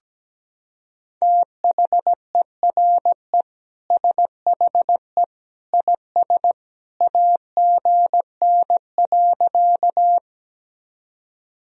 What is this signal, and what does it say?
Morse code, heard as a keyed tone, about 17 wpm, 700 Hz. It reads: THERE SHE IS AGN.
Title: Sentences from Top 200 Words and Abbreviations 17wpm